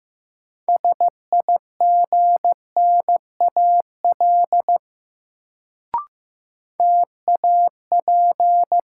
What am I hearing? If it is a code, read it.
Morse code, heard as a keyed tone, about 15 wpm, 700 Hz. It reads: SIGNAL TAP
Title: Sentences from Top 1400 Words - New Word Drill - 15wpm